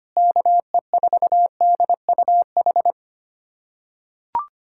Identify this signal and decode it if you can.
Morse code, heard as a keyed tone, about 25 wpm, 700 Hz. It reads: KE4DUH
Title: USA Callsign DE Callsign - 6 Characters - 25wpm